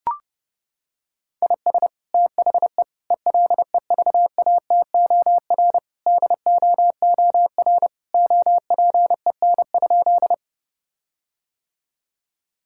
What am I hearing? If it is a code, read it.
Morse code, heard as a keyed tone, about 30 wpm, 700 Hz. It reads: IS THE ELEVATOR DOOR OPEN?
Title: Sentences from Top 1600 Words 30wpm